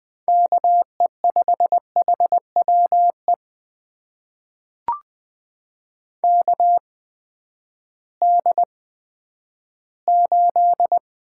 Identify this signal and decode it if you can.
Morse code, heard as a keyed tone, about 20 wpm, spaced as for 11 wpm, 700 Hz. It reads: KE5HWE KD8
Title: USA Call Signs - 6 Characters - 8x Character Spacing - 20wpm